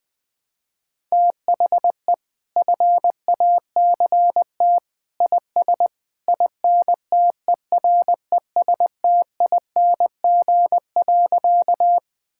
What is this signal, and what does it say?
Morse code, heard as a keyed tone, about 21 wpm, 700 Hz. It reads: THE FACT IS INTERESTING.
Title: Sentences with 2-Word Phrases - 20wpm